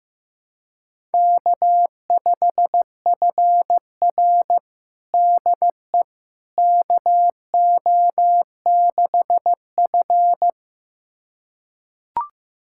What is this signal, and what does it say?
Morse code, heard as a keyed tone, about 15 wpm, 700 Hz. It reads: K5FR DE KO6F
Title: USA Callsign DE Callsign - 4 Characters - 15wpm